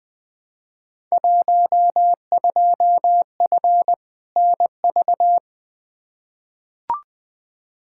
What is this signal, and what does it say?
Morse code, heard as a keyed tone, about 20 wpm, 700 Hz. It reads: 12F NV
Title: Field Day Exchanges Encore - 20wpm